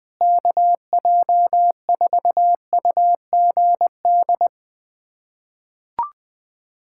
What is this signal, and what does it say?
Morse code, heard as a keyed tone, about 21 wpm, 700 Hz. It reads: KJ4UGD
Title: USA Callsign DE Callsign 20wpm